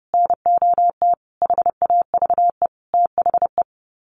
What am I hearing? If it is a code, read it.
Morse code, heard as a keyed tone, about 30 wpm, 700 Hz. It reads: NOT HAVE THE
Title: Sets of 5 Words 30wpm